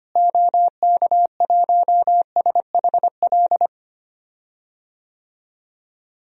Morse code, heard as a keyed tone, about 25 wpm, 700 Hz. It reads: OK1SHL
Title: DXCC-100 Call Signs - 6 Characters - Repeated 3x - 25wpm